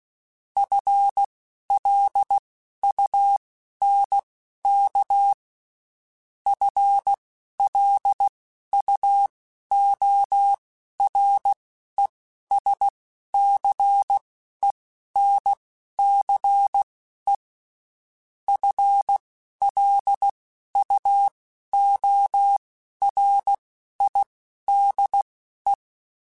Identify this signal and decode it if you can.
Morse code, heard as a keyed tone, about 16 wpm, spaced as for 11 wpm, 800 Hz. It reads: FLUNK FLUORESCENCE FLUORIDE
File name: Spaced / FL Runs sp mp3s / FL Runs sp @15